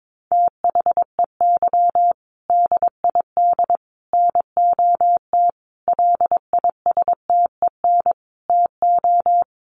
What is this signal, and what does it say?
Morse code, heard as a keyed tone, about 22 wpm, 700 Hz. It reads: THEY DID NOT LISTEN TO
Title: Sentences from Top 600 Words 22wpm